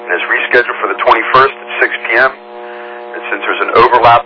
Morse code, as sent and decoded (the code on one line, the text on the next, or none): none